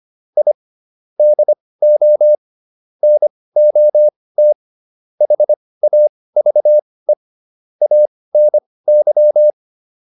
Morse code, heard as a keyed tone, about 25 wpm, spaced as for 18 wpm, 600 Hz. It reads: I DO NOT HAVE ANY